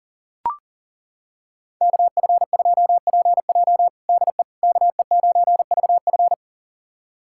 Morse code, heard as a keyed tone, about 40 wpm, 700 Hz. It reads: KF2PJ DE KE9VF
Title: USA Callsign DE Callsign - 5 Characters - 40wpm